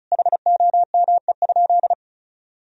Morse code, heard as a keyed tone, about 34 wpm, 700 Hz. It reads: HOME?
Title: Sentences from Top 1000 Words - Review Part 2 - 35wpm